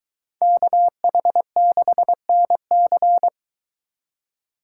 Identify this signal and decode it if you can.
Morse code, heard as a keyed tone, about 22 wpm, 700 Hz. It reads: KH6NC